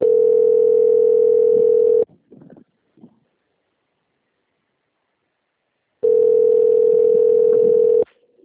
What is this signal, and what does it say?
Ringback tone (North America), 440 + 480 Hz.
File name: out- - - - - .27067